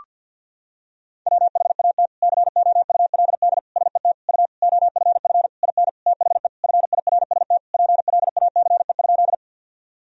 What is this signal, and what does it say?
Morse code, heard as a keyed tone, about 52 wpm, 700 Hz. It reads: WHAT COULD SET U OFF IN THE FIRST PLACE?
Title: Sentences from Top 200 Words and Abbreviations 50wpm